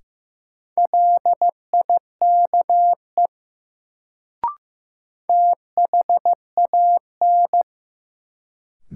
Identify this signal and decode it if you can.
Morse code, heard as a keyed tone, about 15 wpm, 700 Hz. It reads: LIKE THAN
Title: Sentences from Top 100 Words - New Word Drill - 15wpm